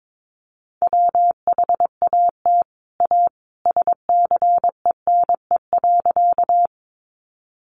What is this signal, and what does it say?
Morse code, heard as a keyed tone, about 22 wpm, 700 Hz. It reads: WHAT A SCENE.
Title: Sentences from Top 1100 Words 22wpm